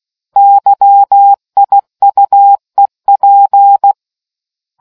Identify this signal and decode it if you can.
Morse code, heard as a keyed tone, about 16 wpm, 800 Hz. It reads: YIUEP